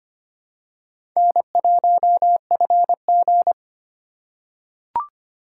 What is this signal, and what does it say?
Morse code, heard as a keyed tone, about 25 wpm, 700 Hz. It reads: N1FG
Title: USA Call Signs - 4 Characters - 4x Character Spacing - 25wpm